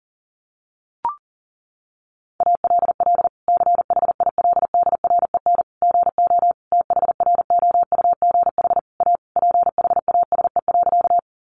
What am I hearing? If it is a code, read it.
Morse code, heard as a keyed tone, about 40 wpm, 700 Hz. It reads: ALL CHILDREN GO THROUGH A PHASE.
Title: Sentences from Top 1200 Words 40wpm